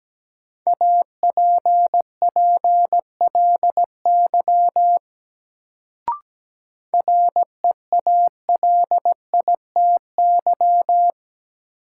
Morse code, heard as a keyed tone, about 17 wpm, 700 Hz. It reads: APPLY REALITY
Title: Sentences from Top 1200 Words - New Word Drill Alt - 17wpm